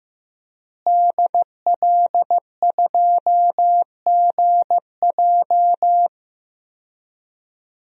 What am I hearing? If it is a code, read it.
Morse code, heard as a keyed tone, about 15 wpm, 700 Hz. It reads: DL2GJ
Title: DXCC-100 Call Signs - 5 Characters - Repeated 2x - 15wpm